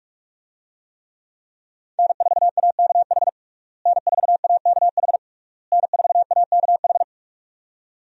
Morse code, heard as a keyed tone, about 46 wpm, 700 Hz. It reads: N4AKH N4AKH N4AKH
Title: USA Call Signs - 5 Characters - Repeated 3x - 45wpm